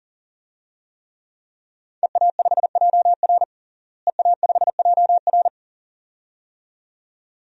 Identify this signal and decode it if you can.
Morse code, heard as a keyed tone, about 40 wpm, 700 Hz. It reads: EA5JR EA5JR
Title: DXCC-100 Call Signs - 5 Characters - Repeated 2x - 40wpm